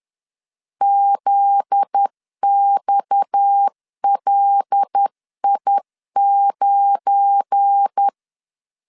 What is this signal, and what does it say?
Morse code, heard as a keyed tone, about 10 wpm, 800 Hz. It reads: ZXLI9
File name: LEZIONI / 9 28 10wpm 2842